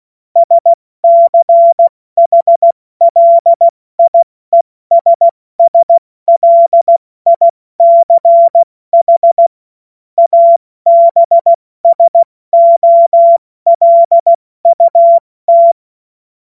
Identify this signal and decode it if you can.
Morse code, heard as a keyed tone, about 16 wpm, 680 Hz. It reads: SCHLIESSLICH ABSOLUT
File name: Morsekurs / Witze / Witz 29. Juni 2021, 70 F80